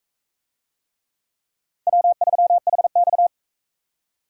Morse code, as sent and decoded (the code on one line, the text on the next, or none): .-- ...-- .... -..-
W3HX